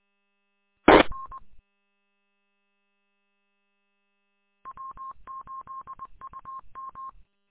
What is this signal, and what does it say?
Morse code, heard as a keyed tone, about 20 wpm, 1070 Hz. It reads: TN W8UM
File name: w8um- -combined